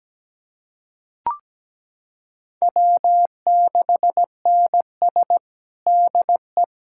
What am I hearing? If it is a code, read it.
Morse code, heard as a keyed tone, about 17 wpm, 700 Hz. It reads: W6NS DE
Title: K1USN SST QSO Practice 17wpm